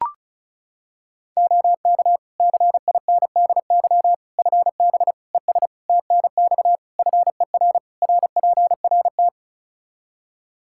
Morse code, heard as a keyed tone, about 34 wpm, 700 Hz. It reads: OK CINDY FB ES TNX FER RPRT
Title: Practice QSO - Exchange 2 - Info - 35wpm